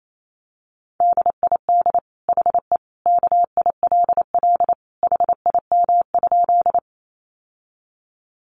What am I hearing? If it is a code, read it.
Morse code, heard as a keyed tone, about 28 wpm, 700 Hz. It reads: DID HE KILL HIM?
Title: Sentences from Top 1400 Words 28wpm